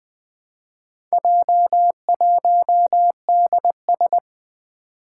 Morse code, heard as a keyed tone, about 20 wpm, 700 Hz. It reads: J1DS
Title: Four Challenging Letter-Number Combinations 20wpm